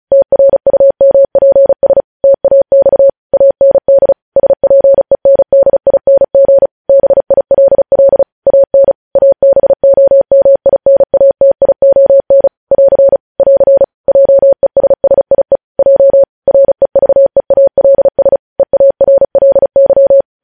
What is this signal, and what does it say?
Morse code, heard as a keyed tone, about 36 wpm, 550 Hz. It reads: TRUMPS TAX AND SPENDING BILL AN ABOMINATION <AR> <AR> JESSIE J REVEALS EARLY
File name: podcast / News headlines-1x-35x35